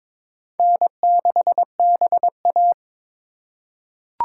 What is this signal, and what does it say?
Morse code, heard as a keyed tone, about 22 wpm, 700 Hz. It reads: N6BA
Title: USA Call Signs - 4 Characters - 3x Character Spacing - 22wpm